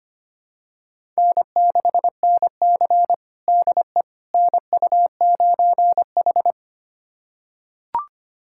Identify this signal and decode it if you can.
Morse code, heard as a keyed tone, about 25 wpm, 700 Hz. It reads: N6NC DE NU9H
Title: USA Callsign DE Callsign - 4 Characters - 25wpm